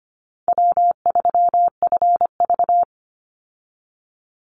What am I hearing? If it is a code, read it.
Morse code, heard as a keyed tone, about 25 wpm, 700 Hz. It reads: W3FV
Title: Easy Call Signs Repeated 3x - 25wpm